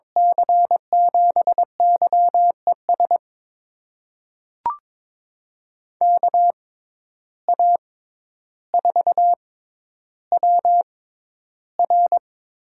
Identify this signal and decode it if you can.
Morse code, heard as a keyed tone, about 22 wpm, spaced as for 11 wpm, 700 Hz. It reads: C7YES KA4WR
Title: USA Call Signs - 6 Characters - 6x Character Spacing - 22wpm